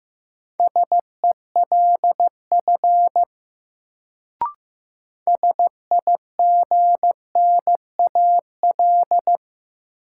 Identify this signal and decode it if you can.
Morse code, heard as a keyed tone, about 15 wpm, 700 Hz. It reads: SELF SIGNAL
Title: Sentences from Top 1400 Words - New Word Drill Alt - 15wpm